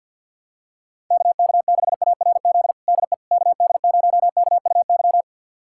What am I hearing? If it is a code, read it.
Morse code, heard as a keyed tone, about 51 wpm, 700 Hz. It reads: KK6ARZ DE KD0KUY
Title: USA Callsign DE Callsign 50wpm